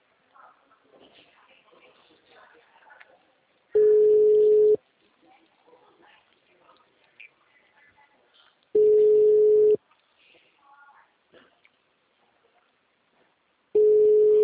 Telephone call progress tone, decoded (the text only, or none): ringback tone